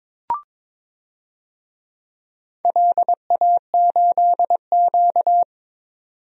Morse code, heard as a keyed tone, about 22 wpm, 700 Hz. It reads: LA8Q
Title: DXCC-100 Call Signs - 4 Characters - 22wpm